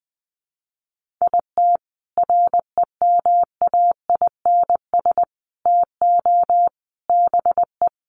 Morse code, heard as a keyed tone, about 20 wpm, 700 Hz. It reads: IT REMAINS TO BE
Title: Sets of 5 Words Encore 20wpm